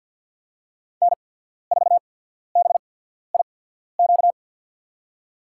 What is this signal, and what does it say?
Morse code, heard as a keyed tone, about 53 wpm, spaced as for 14 wpm, 700 Hz. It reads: N4BIQ
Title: USA Call Signs - 5 Characters - 8x Character Spacing - 50wpm